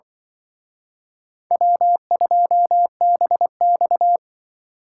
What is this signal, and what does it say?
Morse code, heard as a keyed tone, about 24 wpm, 700 Hz. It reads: W2BX
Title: Speed Racing - USA 1x2 Call Signs 20wpm